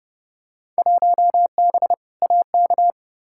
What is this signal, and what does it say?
Morse code, heard as a keyed tone, about 30 wpm, 700 Hz. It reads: 1B AK
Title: Field Day Exchanges Encore - 30wpm